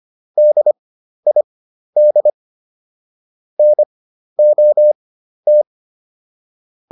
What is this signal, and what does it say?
Morse code, heard as a keyed tone, about 25 wpm, spaced as for 12 wpm, 600 Hz. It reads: DID NOT